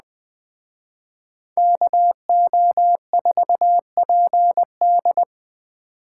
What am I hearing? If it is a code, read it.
Morse code, heard as a keyed tone, about 20 wpm, 700 Hz. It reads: KO4PD